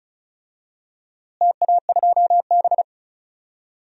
Morse code, heard as a keyed tone, about 36 wpm, 700 Hz. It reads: TA2B
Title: DXCC-100 Call Signs - 4 Characters - Repeated 2x - 35wpm